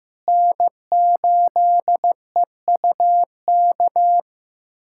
Morse code, heard as a keyed tone, about 16 wpm, 700 Hz. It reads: N8EUK
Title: USA Callsign DE Callsign - 5 Characters - 15wpm